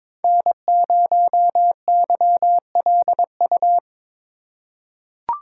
Morse code, heard as a keyed tone, about 22 wpm, 700 Hz. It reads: N0YLU
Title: USA Call Signs - 5 Characters - 2x Character Spacing - 22wpm